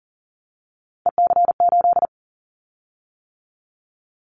Morse code, heard as a keyed tone, about 40 wpm, 700 Hz. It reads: EC8
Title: Callsign Prefixes for Top 50 DXCC - 40wpm